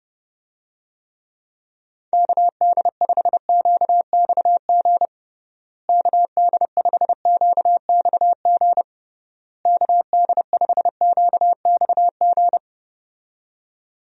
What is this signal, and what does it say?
Morse code, heard as a keyed tone, about 30 wpm, 700 Hz. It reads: KD5QXG KD5QXG KD5QXG
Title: USA Call Signs - 6 Characters - Repeated 3x - 30wpm